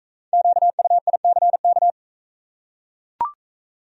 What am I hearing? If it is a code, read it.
Morse code, heard as a keyed tone, about 41 wpm, 700 Hz. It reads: QUICK